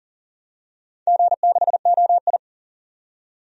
Morse code, heard as a keyed tone, about 41 wpm, 700 Hz. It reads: G6OI
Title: DXCC-100 Call Signs - 4 Characters - 40wpm